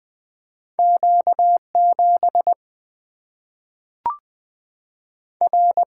Morse code, heard as a keyed tone, about 20 wpm, 700 Hz. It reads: Q7 R
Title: Single-Letter Single-Number 20wpm